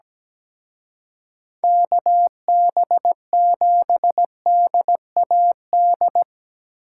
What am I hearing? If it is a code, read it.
Morse code, heard as a keyed tone, about 19 wpm, 700 Hz. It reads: KB7DAD
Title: Speed Racing - USA 2x3 Call Signs 17wpm